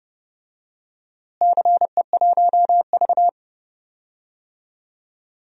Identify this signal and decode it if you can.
Morse code, heard as a keyed tone, about 32 wpm, 700 Hz. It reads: CE1V
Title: DXCC-100 Call Signs - 4 Characters - Repeated 2x - 30wpm